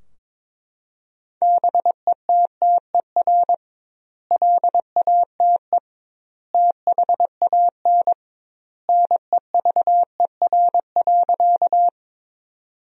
Morse code, heard as a keyed tone, about 22 wpm, 700 Hz. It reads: BETTER LATE THAN NEVER.
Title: 2x Word-Spacing - Sentences from Top 400 Words - 22wpm